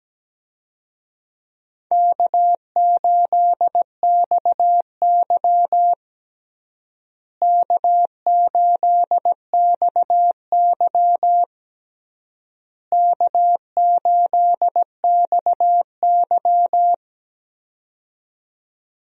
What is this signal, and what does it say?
Morse code, heard as a keyed tone, about 17 wpm, 700 Hz. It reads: K8XY K8XY K8XY
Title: USA Call Signs - 4 Characters - Repeated 3x - 17wpm